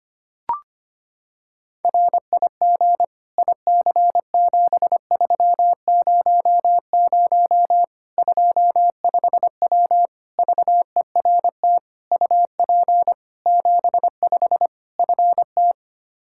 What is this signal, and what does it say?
Morse code, heard as a keyed tone, about 25 wpm, 700 Hz. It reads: RIG IC7300 25W VERT UP 75 FT
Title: Practice QSO - Exchange 2 - Info - 25wpm